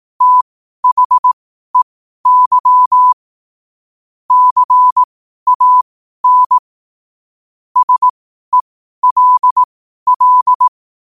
Morse code, heard as a keyed tone, about 18 wpm, spaced as for 12 wpm, 1000 Hz. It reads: THEY CAN SELL
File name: qotdfiles / 12wpm